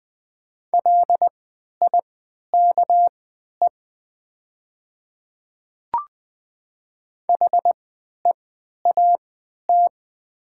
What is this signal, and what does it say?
Morse code, heard as a keyed tone, about 20 wpm, spaced as for 11 wpm, 700 Hz. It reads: LIKE HEAT
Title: 3x Character Spacing - Top 500 Words - 20wpm